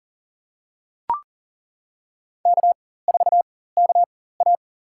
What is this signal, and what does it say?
Morse code, heard as a keyed tone, about 40 wpm, spaced as for 19 wpm, 700 Hz. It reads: K4KA
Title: USA Call Signs - 4 Characters - 4x Character Spacing - 40wpm